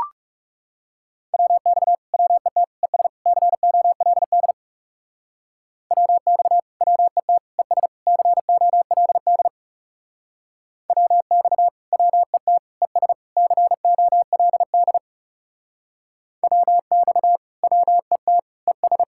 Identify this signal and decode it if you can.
Morse code, heard as a keyed tone, about 36 wpm, 700 Hz. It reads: WX WET ES COLD WX WET ES COLD WX WET ES COLD WX WET ES
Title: Speed-Racing Common QSO Phrases - 30wpm